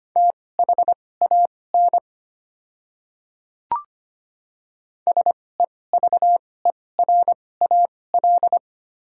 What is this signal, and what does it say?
Morse code, heard as a keyed tone, about 25 wpm, spaced as for 18 wpm, 690 Hz. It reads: THAN SEVERAL
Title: 2x Character Spacing - Top 500 Words - 25wpm